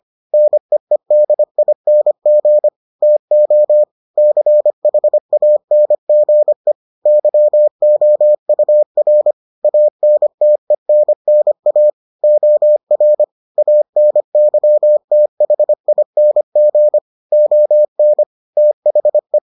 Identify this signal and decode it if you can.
Morse code, heard as a keyed tone, about 25 wpm, 600 Hz. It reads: NEEDING TO CHANGE YOUR ANTENNA OR ANYTHING ON THE